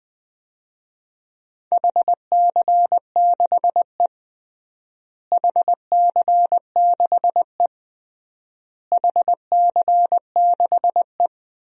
Morse code, heard as a keyed tone, about 20 wpm, 700 Hz. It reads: HC6E HC6E HC6E
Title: DXCC-100 Call Signs - 4 Characters - Repeated 3x - 20wpm